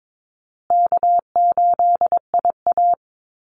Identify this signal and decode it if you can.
Morse code, heard as a keyed tone, about 22 wpm, 700 Hz. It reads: K8IA
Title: Easy Call Signs Repeated 3x - 22wpm